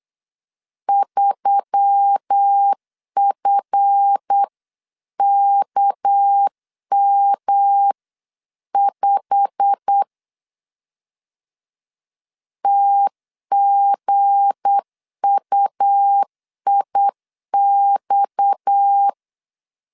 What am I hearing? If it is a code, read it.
Morse code, heard as a keyed tone, about 8 wpm, 800 Hz. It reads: 3FKM5 TGUIX
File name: LEZIONI / 9 2 8wpm 3437